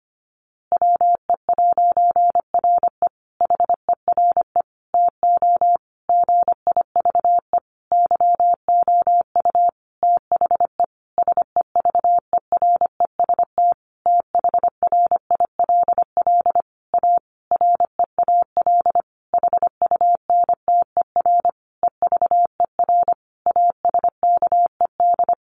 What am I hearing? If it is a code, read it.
Morse code, heard as a keyed tone, about 25 wpm, 700 Hz. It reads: WE'RE HERE TO GIVE YOU THE SEVEREST THRILL A REAL HUNTER EVER ASKED
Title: A Sound Of Thunder 25wpm